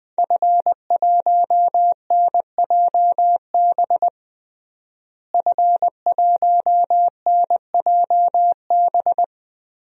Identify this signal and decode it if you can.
Morse code, heard as a keyed tone, about 20 wpm, 700 Hz. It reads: F1NJB F1NJB
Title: DXCC-100 Call Signs - 5 Characters - Repeated 2x - 20wpm